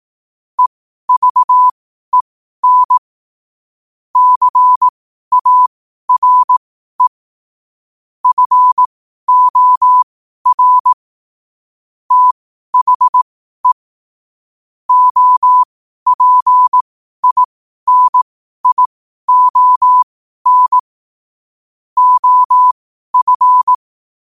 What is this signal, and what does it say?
Morse code, heard as a keyed tone, about 18 wpm, spaced as for 12 wpm, 1000 Hz. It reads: EVEN CARE FOR THE OPINION OF
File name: qotdfiles / 12wpm